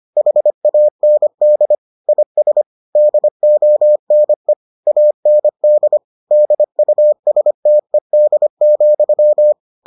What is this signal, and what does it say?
Morse code, heard as a keyed tone, about 25 wpm, 600 Hz. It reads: HAND IS DONE AND DUSTED,